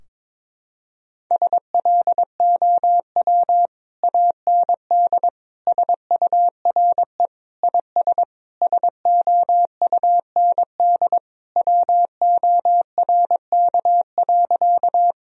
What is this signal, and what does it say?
Morse code, heard as a keyed tone, about 21 wpm, 700 Hz. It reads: SLOW AND SURE IS SOUND WORK.
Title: Sentences from Top 1000 Words - Review Part 2 - 22wpm